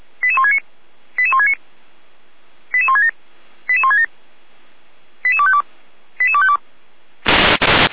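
Five-tone selective call (ZVEI), heard as quick runs of five tones, tones of about 70 ms. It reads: 88169 88169 88167 88167 88252 88252
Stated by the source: ZVEI five-tone selective call 88447 received by SDR.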